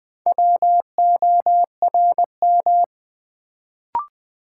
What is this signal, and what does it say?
Morse code, heard as a keyed tone, about 20 wpm, 700 Hz. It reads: WORM
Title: Build A Word - 1 Letter At A Time - 4-Letter Words - 20wpm